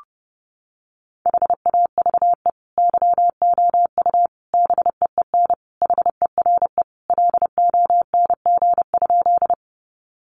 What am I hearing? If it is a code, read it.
Morse code, heard as a keyed tone, about 30 wpm, 700 Hz. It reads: HAVE YOU BEEN HERE LONG?
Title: Sentences from Top 200 Words 30wpm